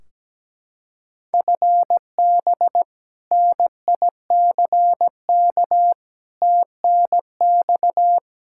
Morse code, heard as a keyed tone, about 17 wpm, 700 Hz. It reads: FB NICK TNX
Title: Practice QSO - Exchange 3 - More Info - 17wpm